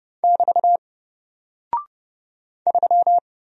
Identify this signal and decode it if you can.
Morse code, heard as a keyed tone, about 30 wpm, 700 Hz. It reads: <BT> 3
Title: Single Character (Letter:Number:Punctuation:Prosign) Alt 30wpm